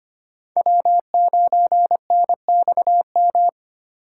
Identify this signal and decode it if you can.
Morse code, heard as a keyed tone, about 33 wpm, 700 Hz. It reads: W9NXM